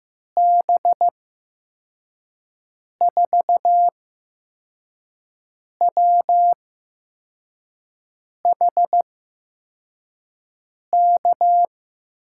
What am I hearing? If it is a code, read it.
Morse code, heard as a keyed tone, about 15 wpm, spaced as for 4 wpm, 700 Hz. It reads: B4WHK
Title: USA Call Signs - 6 Characters - 8x Character Spacing - 15wpm